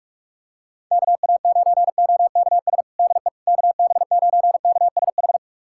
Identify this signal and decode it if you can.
Morse code, heard as a keyed tone, about 45 wpm, 690 Hz. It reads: KA9OKS DE KB9KSH
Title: USA Callsign DE Callsign - 6 Characters - 45wpm